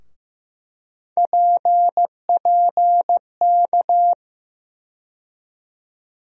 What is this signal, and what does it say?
Morse code, heard as a keyed tone, about 15 wpm, 700 Hz. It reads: PPK
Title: USA Call Sign Postfixes - 3 Letters 15wpm